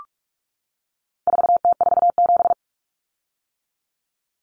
Morse code, heard as a keyed tone, about 44 wpm, 700 Hz. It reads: 4T47
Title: Common Cut Numbers - 4-Digit 45wpm